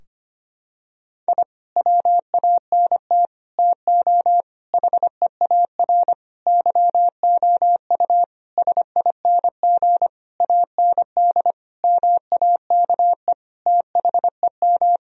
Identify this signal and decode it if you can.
Morse code, heard as a keyed tone, about 25 wpm, 700 Hz. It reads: I WANT TO HEAR YOU SING AND MAKE THEM
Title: Sentences from Top 1000 Words - Review Part 4 - 25wpm